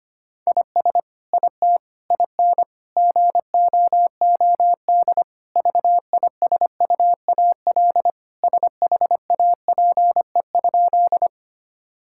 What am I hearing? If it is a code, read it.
Morse code, heard as a keyed tone, about 25 wpm, 700 Hz. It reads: IS IT IN GOOD VISUAL SHAPE?
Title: Sentences from Top 1000 Words - Review Part 6 - 25wpm